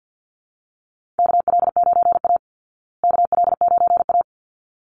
Morse code, heard as a keyed tone, about 50 wpm, 700 Hz. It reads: KL9A KL9A
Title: Call Signs Repeated 3x - 50wpm